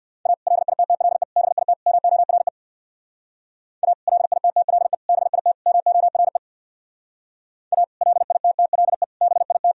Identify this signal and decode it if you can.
Morse code, heard as a keyed tone, about 59 wpm, 700 Hz. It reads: A LITTLE BIT MORE A LITTLE BIT MORE A LITTLE BIT
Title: Speed-Racing - Sets of 4 Words 50wpm